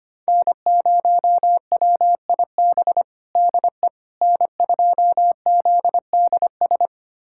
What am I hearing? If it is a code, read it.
Morse code, heard as a keyed tone, about 25 wpm, 700 Hz. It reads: N0WIB DE N2ZDS
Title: USA Callsign DE Callsign - 5 Characters - 25wpm